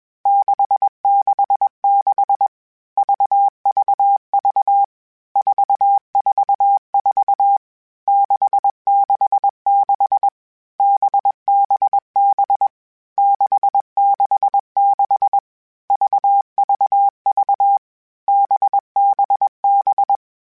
Morse code, heard as a keyed tone, about 21 wpm, 800 Hz. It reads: 666 VVV 444 666 BBB 666 VVV BBB
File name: Basic Sets / Basic Letter Groups Collective7 / VB46@20